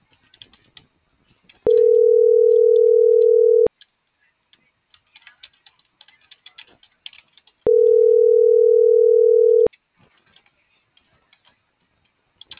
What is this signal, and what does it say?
Ringback tone (North America), 440 + 480 Hz.
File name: CENTRAL TERRESTRE / out ramon.lamorena.2024